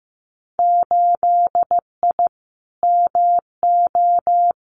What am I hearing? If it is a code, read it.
Morse code, heard as a keyed tone, about 15 wpm, 700 Hz. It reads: 8I MO
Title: Winter Field Day Exchanges Encore 15wpm